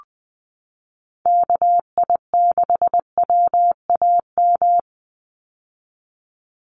Morse code, heard as a keyed tone, about 20 wpm, 700 Hz. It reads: KI6WAM
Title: USA Call Signs 20wpm